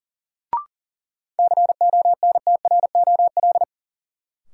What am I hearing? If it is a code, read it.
Morse code, heard as a keyed tone, about 39 wpm, 700 Hz. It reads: CONTROL
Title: Sentences from Top 1000 Words - Review Part 2 - New Word Drill - 40wpm